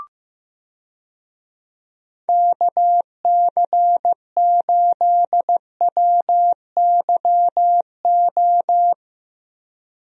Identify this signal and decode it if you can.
Morse code, heard as a keyed tone, about 15 wpm, 700 Hz. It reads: KC8WYO
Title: USA Call Signs - 6 Characters - Repeated 2x - 15wpm